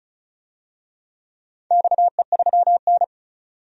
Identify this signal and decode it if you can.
Morse code, heard as a keyed tone, about 36 wpm, 700 Hz. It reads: XE3N
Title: DXCC-100 Call Signs - 4 Characters - 35wpm